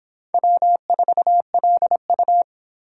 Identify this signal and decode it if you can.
Morse code, heard as a keyed tone, about 30 wpm, 700 Hz. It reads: W4LU